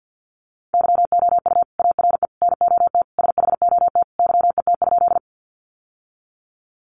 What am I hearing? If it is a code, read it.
Morse code, heard as a keyed tone, about 50 wpm, 700 Hz. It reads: YOU ARE NOT SHOT YET?
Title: Sentences from Top 1100 Words 50wpm